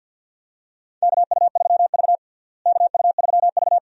Morse code, heard as a keyed tone, about 46 wpm, 700 Hz. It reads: KU3V KU3V
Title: USA Call Signs - 4 Characters - Repeated 2x - 50wpm